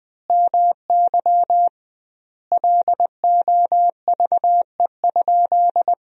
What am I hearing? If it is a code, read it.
Morse code, heard as a keyed tone, about 19 wpm, 700 Hz. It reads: MY LOVE?
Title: 2x Word-Spacing - Sentences from Top 500 Words - 20wpm